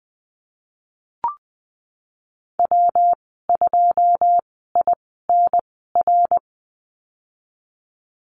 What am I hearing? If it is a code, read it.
Morse code, heard as a keyed tone, about 19 wpm, spaced as for 14 wpm, 700 Hz. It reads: W2INR
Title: 122 Lesson 20g - Number 3 - Call Signs 20wpm